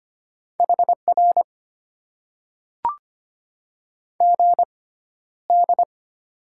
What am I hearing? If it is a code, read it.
Morse code, heard as a keyed tone, about 25 wpm, spaced as for 13 wpm, 690 Hz. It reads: HR GD
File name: Top 90 QSO Elements - 6x Character Spacing - 25wpm